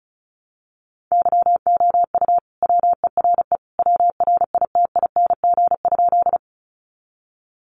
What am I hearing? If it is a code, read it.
Morse code, heard as a keyed tone, about 34 wpm, 700 Hz. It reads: YOU WERE WRITING?
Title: Sentences from Top 800 Words 35wpm